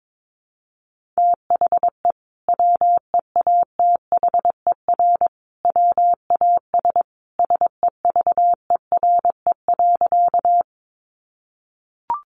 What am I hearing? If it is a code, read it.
Morse code, heard as a keyed tone, about 22 wpm, 700 Hz. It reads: THE WEATHER WAS SEVERE.
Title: Sentences from Top 1400 Words 22wpm